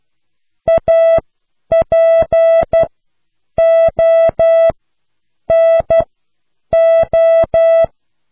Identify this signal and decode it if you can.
Morse code, heard as a keyed tone, about 12 wpm, spaced as for 8 wpm, 660 Hz. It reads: APONO